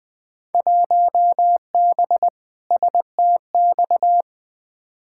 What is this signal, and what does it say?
Morse code, heard as a keyed tone, about 21 wpm, 700 Hz. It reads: 1B STX
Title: Field Day Exchanges Encore - 20wpm